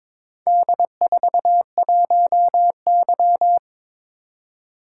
Morse code, heard as a keyed tone, about 22 wpm, 690 Hz. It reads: D41Y
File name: practice / 22wpm / Four Letter-Number Combinations - 22wpm